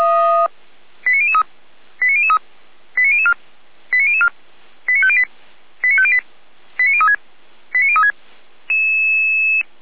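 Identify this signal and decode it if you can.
Five-tone selective call (ZVEI), heard as quick runs of five tones, tones of about 70 ms. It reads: 89003 89003 89004 89004 89508 89508 89036 89036
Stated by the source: ZVEI five-tone call received on SDR, feuerwehr-orlamuende (fire brigade Orlamünde).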